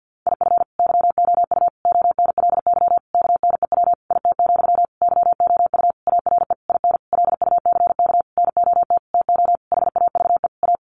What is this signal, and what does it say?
Morse code, heard as a keyed tone, about 48 wpm, 700 Hz. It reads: IF YOU ONLY KNEW IT, YOU ARE IN LUCK NOT TO HAVE A
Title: Sentences from Top 1500 Words 50wpm